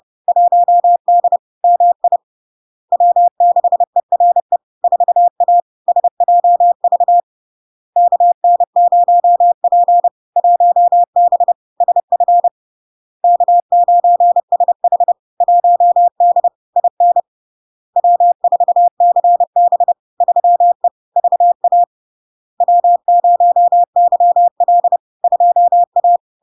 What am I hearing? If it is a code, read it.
Morse code, heard as a keyed tone, about 30 wpm, 700 Hz. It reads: 1D MI W6ERE 4A SJV KN0P 1B SF K9SH 1D IN W4CB 3E VA W0YL 2A
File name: LARK-FD-2022-Log-in-CW-30wpm-700-Hz